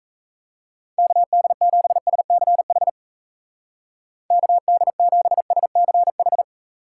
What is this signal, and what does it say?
Morse code, heard as a keyed tone, about 34 wpm, 700 Hz. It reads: KD7SCH KD7SCH
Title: Speed Racing - USA 2x3 Call Signs 28wpm